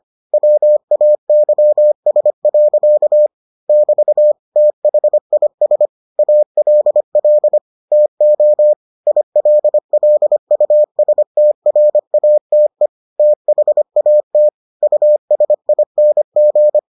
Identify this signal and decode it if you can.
Morse code, heard as a keyed tone, about 25 wpm, 600 Hz. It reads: WAYS. <BT> THIS ALL TO ILLUSTRATE THAT USING